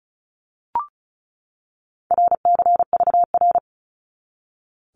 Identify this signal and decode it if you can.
Morse code, heard as a keyed tone, about 36 wpm, 700 Hz. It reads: RCVR
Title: Abbreviations 35wpm